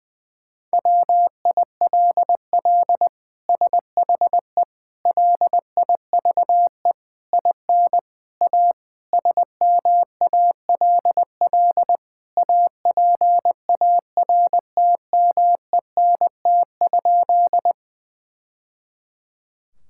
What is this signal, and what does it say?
Morse code, heard as a keyed tone, about 20 wpm, 700 Hz. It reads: WILL SHE LIVE IN A SMALL APARTMENT?
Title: Sentences from Top 1000 Words - Review Part 6 - 20wpm